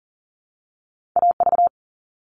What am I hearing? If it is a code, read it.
Morse code, heard as a keyed tone, about 42 wpm, 700 Hz. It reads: AV
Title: All Cut Numbers - 2-Digit 40wpm